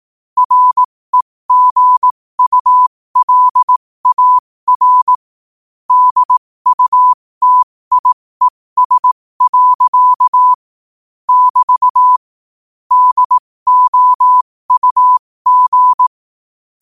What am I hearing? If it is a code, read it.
Morse code, heard as a keyed tone, about 18 wpm, spaced as for 15 wpm, 1000 Hz. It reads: REGULAR DUTIES. <BT> DOUG
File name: qotdfiles / 15wpm